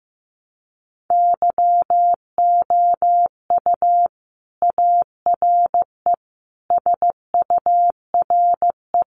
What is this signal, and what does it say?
Morse code, heard as a keyed tone, about 15 wpm, 700 Hz. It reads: YOU ARE SURE
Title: Sentences from Top 300 Words 15wpm